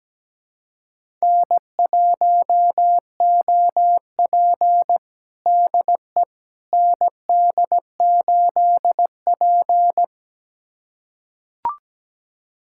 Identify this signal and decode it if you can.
Morse code, heard as a keyed tone, about 17 wpm, 700 Hz. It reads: N1OP DE ND8P
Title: USA Callsign DE Callsign - 4 Characters - 17wpm